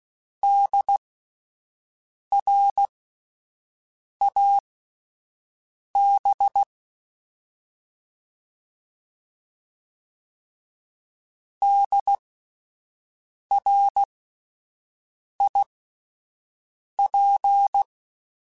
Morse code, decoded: DRAB DRIP

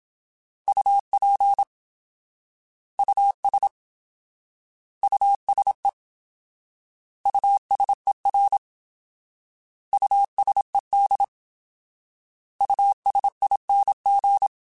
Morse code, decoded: UP US USE USER USED USING

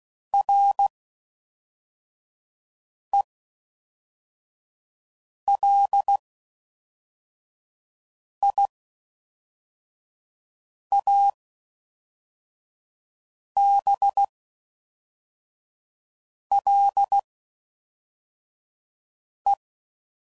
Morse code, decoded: RELIABLE

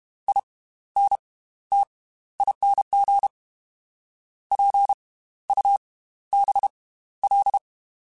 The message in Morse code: INTING PUBL

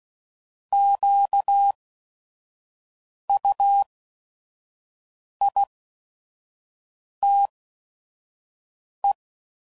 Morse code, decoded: QUITE

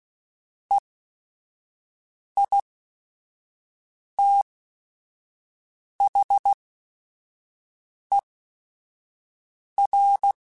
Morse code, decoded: EITHER

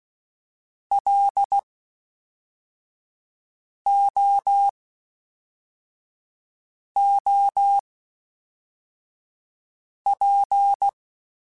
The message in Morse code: LOOP

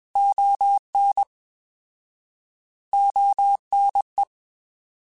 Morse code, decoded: ON ONE

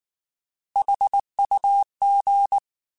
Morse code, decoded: HUG

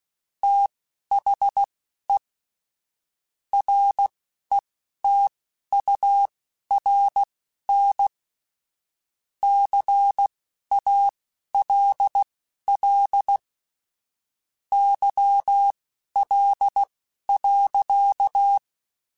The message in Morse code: THE RETURN CALL YL.

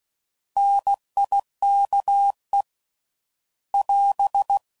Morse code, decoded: NIKE <AS>

